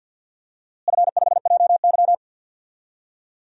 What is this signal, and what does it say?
Morse code, heard as a keyed tone, about 47 wpm, 700 Hz. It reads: F5JY
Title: DXCC-100 Call Signs - 4 Characters - Repeated 2x - 50wpm